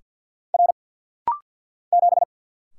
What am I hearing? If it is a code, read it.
Morse code, heard as a keyed tone, about 50 wpm, 700 Hz. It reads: R 7
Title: Single Letter-Number Alt - Rapid-Fire 50wpm